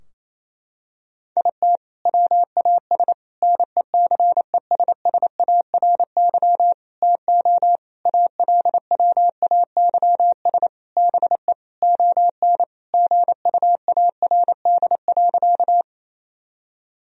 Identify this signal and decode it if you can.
Morse code, heard as a keyed tone, about 28 wpm, 700 Hz. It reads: IT WAS NECESSARY TO ALWAYS BE ON GUARD.
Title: Sentences from Top 1000 Words - Review Part 4 - 28wpm